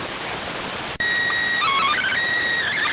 SSTV (slow-scan television), PD 290.